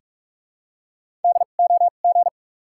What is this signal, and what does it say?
Morse code, heard as a keyed tone, about 42 wpm, spaced as for 31 wpm, 700 Hz. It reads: DOG
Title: 2x Character Spacing - Top 500 Words - 45wpm